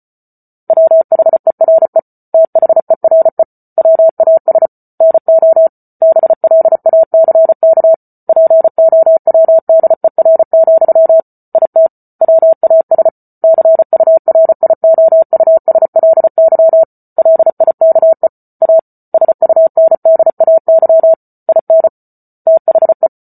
Morse code, heard as a keyed tone, about 34 wpm, 650 Hz. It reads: WHERE THERE WAS NO BLACK POWDER, IT WAS CURIOUSLY LIKE A SUNDAY IN THE